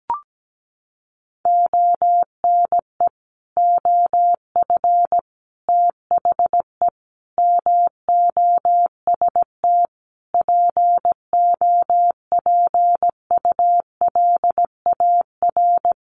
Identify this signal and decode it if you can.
Morse code, heard as a keyed tone, about 17 wpm, 700 Hz. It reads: ONE OF THE MOST POPULAR
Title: Sets of 5 Words 17wpm